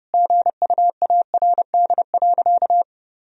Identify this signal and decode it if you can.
Morse code, heard as a keyed tone, about 32 wpm, 700 Hz. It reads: GUARD.
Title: Sentences from Top 1000 Words - Review Part 4 - 30wpm